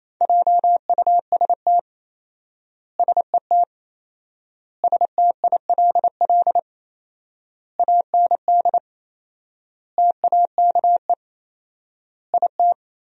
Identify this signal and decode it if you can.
Morse code, heard as a keyed tone, about 28 wpm, 700 Hz. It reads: JUST SET STILL AND TAKE IT
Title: 4x Word-Spacing - Sentences from Top 300 Words - 28wpm